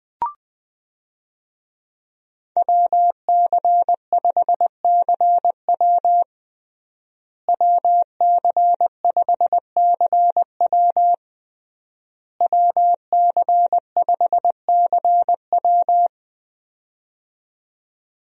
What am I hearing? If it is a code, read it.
Morse code, heard as a keyed tone, about 20 wpm, 700 Hz. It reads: WC5CW WC5CW WC5CW
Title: USA Call Signs - 5 Characters - Repeated 3x - 20wpm